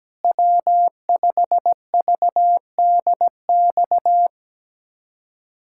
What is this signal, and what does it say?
Morse code, heard as a keyed tone, about 17 wpm, 700 Hz. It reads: W5VDX
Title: USA Callsign DE Callsign 17wpm